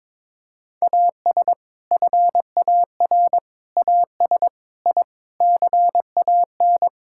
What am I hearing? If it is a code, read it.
Morse code, heard as a keyed tone, about 22 wpm, 700 Hz. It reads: AS FAR AS I CAN
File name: Sentences with 7-Word Phrases - 22wpm